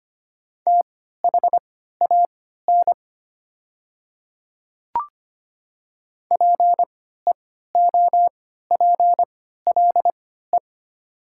Morse code, heard as a keyed tone, about 25 wpm, spaced as for 14 wpm, 700 Hz. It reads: THAN PEOPLE
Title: 3x Character Spacing - Top 100 Words - 25wpm